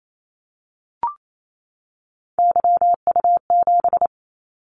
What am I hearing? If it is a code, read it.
Morse code, heard as a keyed tone, about 28 wpm, 700 Hz. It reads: YU7
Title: Callsign Prefixes for Top 25 DXCC - 28wpm